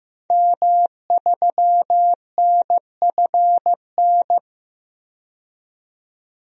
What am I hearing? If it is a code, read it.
Morse code, heard as a keyed tone, about 15 wpm, 700 Hz. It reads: M3NFN
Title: DXCC-100 Call Signs - 5 Characters - 15wpm